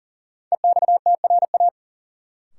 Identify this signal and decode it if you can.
Morse code, heard as a keyed tone, about 40 wpm, 700 Hz. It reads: EXTRA
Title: Sentences from Top 1000 Words - New Word Drill - 40wpm